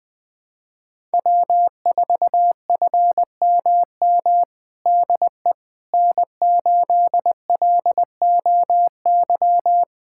Morse code, heard as a keyed tone, about 20 wpm, 700 Hz. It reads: W4FMM DE N8LOY
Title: USA Callsign DE Callsign - 5 Characters - 20wpm